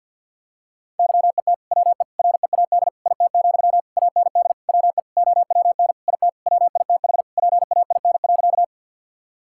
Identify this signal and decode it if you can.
Morse code, heard as a keyed tone, about 50 wpm, 700 Hz. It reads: YET WE READ IT, AND WE OWN IT WITH PAIN.
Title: Sentences from Top 1600 Words 50wpm